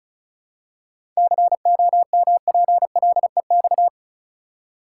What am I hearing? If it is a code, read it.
Morse code, heard as a keyed tone, about 34 wpm, 700 Hz. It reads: COMPLEX
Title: Seven Letter Words - 35wpm